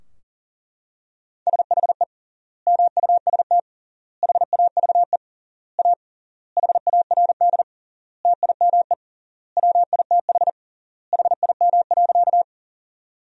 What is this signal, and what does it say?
Morse code, heard as a keyed tone, about 40 wpm, 700 Hz. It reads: SHE MUST HAVE A HARD TIME WITH HIM.
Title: 3x Word-Spacing - Sentences from Top 300 Words - 40wpm